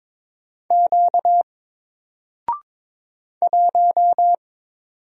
Morse code, heard as a keyed tone, about 22 wpm, 700 Hz. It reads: Q 1
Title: Single Letter-Number Alt 22wpm